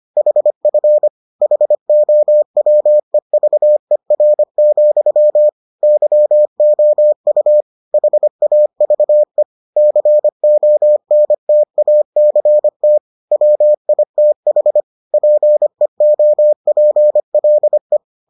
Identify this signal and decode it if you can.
Morse code, heard as a keyed tone, about 25 wpm, 600 Hz. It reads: HF HOWEVER, YOU HAVE CONTACT WITH PEOPLE